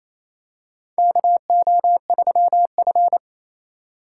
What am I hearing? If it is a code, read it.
Morse code, heard as a keyed tone, about 28 wpm, 700 Hz. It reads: KO3F
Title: USA Call Signs - 4 Characters - 4x Character Spacing - 28wpm